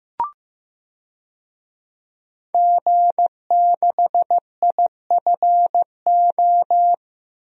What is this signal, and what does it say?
Morse code, heard as a keyed tone, about 15 wpm, 700 Hz. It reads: G6IFO
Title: DXCC-100 Call Signs - 5 Characters - Repeated 3x - 15wpm